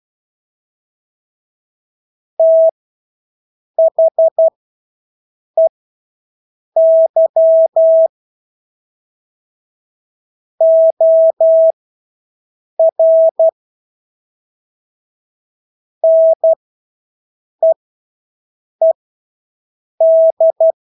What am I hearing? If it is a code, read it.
Morse code, heard as a keyed tone, about 12 wpm, spaced as for 6 wpm, 650 Hz. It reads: THEY OR NEED
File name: Engelske ord 12-6 wpm